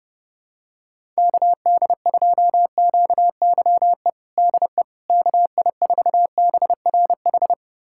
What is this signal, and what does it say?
Morse code, heard as a keyed tone, about 32 wpm, 700 Hz. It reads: KD2QYE DE KI4BRH
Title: USA Callsign DE Callsign 30wpm